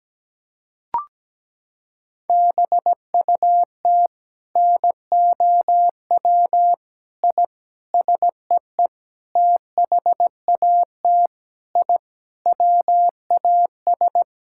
Morse code, heard as a keyed tone, about 17 wpm, 700 Hz. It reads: BUT NOW I SEE THAT I WAS
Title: Sentences from Top 1000 Words - Review Part 6 - 17wpm